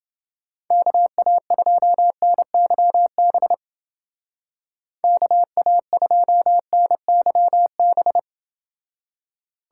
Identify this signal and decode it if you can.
Morse code, heard as a keyed tone, about 29 wpm, 700 Hz. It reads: KA2NYB KA2NYB